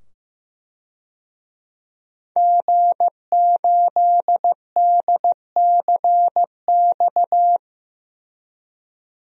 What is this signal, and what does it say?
Morse code, heard as a keyed tone, about 16 wpm, 700 Hz. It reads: G8DCX
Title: DXCC-100 Call Signs - 5 Characters - 15wpm